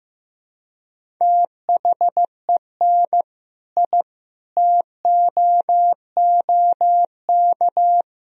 Morse code, heard as a keyed tone, about 15 wpm, 700 Hz. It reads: THEN I TOOK
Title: Sentences from Top 1000 Words - Review Part 1 - 15wpm